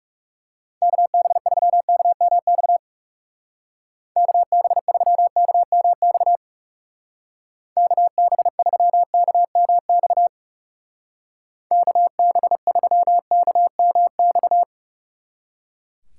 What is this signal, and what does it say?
Morse code, heard as a keyed tone, about 37 wpm, 700 Hz. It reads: KB3KMX KB3KMX KB3KMX KB3KMX
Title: Speed Racing - USA 2x3 Call Signs 30wpm